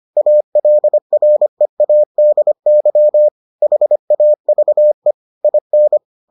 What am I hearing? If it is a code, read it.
Morse code, heard as a keyed tone, about 25 wpm, 600 Hz. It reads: ALREADY HAVE IN